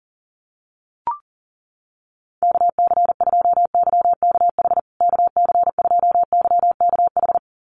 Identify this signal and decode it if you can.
Morse code, heard as a keyed tone, about 40 wpm, 700 Hz. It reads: KC2YKH KC2YKH
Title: USA Call Signs Repeated 2x 40wpm